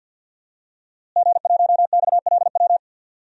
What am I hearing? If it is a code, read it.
Morse code, heard as a keyed tone, about 52 wpm, 700 Hz. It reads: G1XLW
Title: DXCC-100 Call Signs - 5 Characters - Repeated 2x - 50wpm